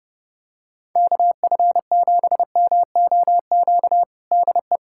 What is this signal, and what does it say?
Morse code, heard as a keyed tone, about 32 wpm, 700 Hz. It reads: KF7MOQ DE
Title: USA Callsign DE Callsign - 6 Characters - 30wpm